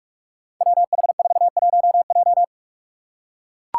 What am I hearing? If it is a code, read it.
Morse code, heard as a keyed tone, about 42 wpm, 700 Hz. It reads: WH41J
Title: Five Challenging Letter-Number Combinations 45wpm